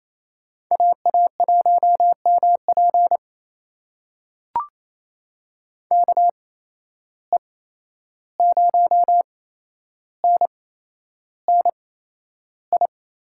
Morse code, heard as a keyed tone, about 28 wpm, spaced as for 11 wpm, 700 Hz. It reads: AA1MP KE0NNI
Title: USA Call Signs - 8x Character Spacing - 28wpm